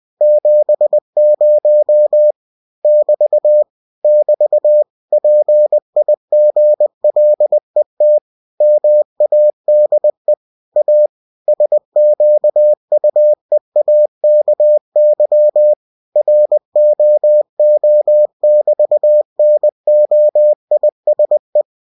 Morse code, heard as a keyed tone, about 20 wpm, 600 Hz. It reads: 70 <BT> <BT> PIGLET MADE A SQUEAKY ROO<BT>NOISE
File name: cw-books / winnie-the-pooh / winnie-20wpm-page-0070